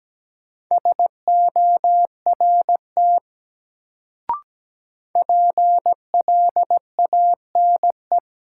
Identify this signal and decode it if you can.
Morse code, heard as a keyed tone, about 17 wpm, 700 Hz. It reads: SORT PLANE
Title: Sentences from Top 1000 Words - Review Part 3 - New Word Drill - 17wpm